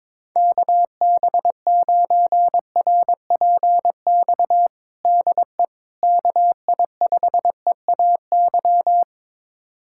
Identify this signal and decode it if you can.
Morse code, heard as a keyed tone, about 23 wpm, 700 Hz. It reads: KB9RPX DE KI5EAY
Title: USA Callsign DE Callsign - 6 Characters - 22wpm